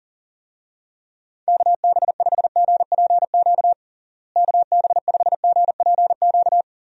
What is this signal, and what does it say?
Morse code, heard as a keyed tone, about 40 wpm, 700 Hz. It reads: KB5GPQ KB5GPQ
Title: USA Call Signs - 6 Characters - Repeated 3x - 40wpm